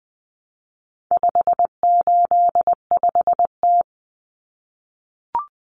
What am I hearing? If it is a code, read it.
Morse code, heard as a keyed tone, about 20 wpm, 700 Hz. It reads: 585T
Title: Common Cut Numbers - 4-Digit 20wpm